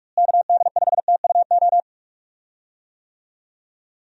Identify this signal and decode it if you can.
Morse code, heard as a keyed tone, about 46 wpm, 700 Hz. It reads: KD5TUO
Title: USA Call Signs - 6 Characters - Repeated 2x - 45wpm